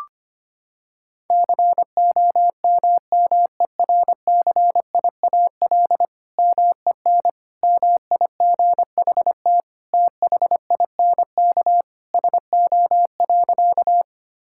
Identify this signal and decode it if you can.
Morse code, heard as a keyed tone, about 25 wpm, 700 Hz. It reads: COMMERCIAL MEN MIGHT THINK SO.
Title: Sentences from Top 1000 Words - Review Part 5 - 25wpm